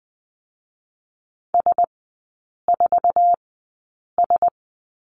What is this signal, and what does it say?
Morse code, heard as a keyed tone, about 20 wpm, spaced as for 8 wpm, 700 Hz. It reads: S4S
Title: 047 Lesson 9e - Number 4 - Three Characters 20wpm